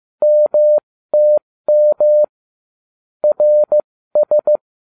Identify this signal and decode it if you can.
Morse code, heard as a keyed tone, about 15 wpm, spaced as for 13 wpm, 600 Hz. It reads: MTM RS
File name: cw / ubacw08 2